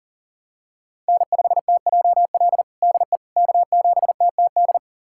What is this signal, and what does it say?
Morse code, heard as a keyed tone, about 37 wpm, 700 Hz. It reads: N5TJL DE K7TTD
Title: USA Callsign DE Callsign - 5 Characters - 40wpm